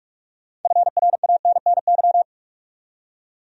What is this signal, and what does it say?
Morse code, heard as a keyed tone, about 46 wpm, 700 Hz. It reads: FRANNY
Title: USA Common Names - 45wpm